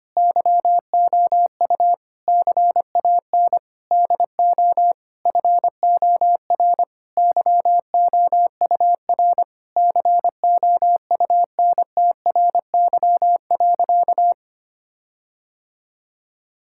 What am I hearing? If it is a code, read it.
Morse code, heard as a keyed tone, about 25 wpm, 700 Hz. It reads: YOU CAN DO FOR YOUR COUNTRY.
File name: practice / 25wpm / Sentences with 9-Word Phrases - 25wpm